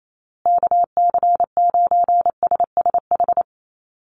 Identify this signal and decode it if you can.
Morse code, heard as a keyed tone, about 28 wpm, 700 Hz. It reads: KC9SSH
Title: USA Call Signs Repeated 2x 28wpm